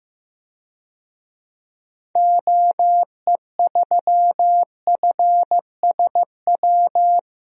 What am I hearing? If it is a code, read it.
Morse code, heard as a keyed tone, about 15 wpm, 700 Hz. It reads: OE3FSW
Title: DXCC-100 Call Signs - 6 Characters - Repeated 3x - 15wpm